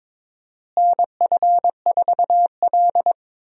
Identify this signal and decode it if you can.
Morse code, heard as a keyed tone, about 23 wpm, 700 Hz. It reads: NF4L
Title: USA Call Signs - 4 Characters - Repeated 3x - 22wpm